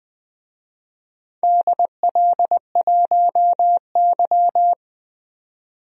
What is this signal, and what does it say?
Morse code, heard as a keyed tone, about 21 wpm, 700 Hz. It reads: DL1Y
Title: DXCC-100 Call Signs - 4 Characters - 20wpm